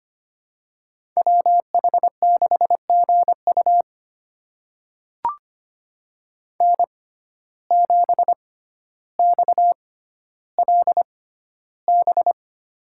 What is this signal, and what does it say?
Morse code, heard as a keyed tone, about 25 wpm, spaced as for 13 wpm, 700 Hz. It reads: WH6GU N7XLB
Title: USA Call Signs - 5 Characters - 6x Character Spacing - 25wpm